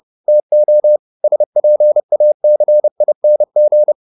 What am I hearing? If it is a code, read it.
Morse code, heard as a keyed tone, about 30 wpm, 600 Hz. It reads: TO SPACING